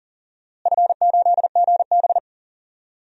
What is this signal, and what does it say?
Morse code, heard as a keyed tone, about 40 wpm, 700 Hz. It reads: F8GB
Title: DXCC-100 Call Signs - 4 Characters - Repeated 3x - 40wpm